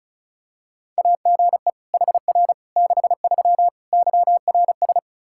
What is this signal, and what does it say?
Morse code, heard as a keyed tone, about 36 wpm, 700 Hz. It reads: AGE HR 63 YRS
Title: Practice QSO - Exchange 3 - More Info - 35wpm